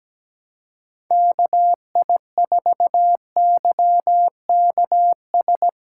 Morse code, heard as a keyed tone, about 18 wpm, 700 Hz. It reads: KI4YKS
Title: Speed Racing - USA 2x3 Call Signs 17wpm